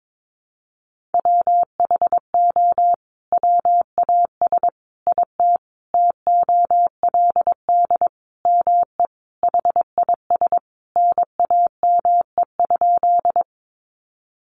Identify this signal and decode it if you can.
Morse code, heard as a keyed tone, about 22 wpm, 700 Hz. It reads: WHO WAS IT TOLD ME HIS NAME?
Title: Sentences from Top 400 Words 22wpm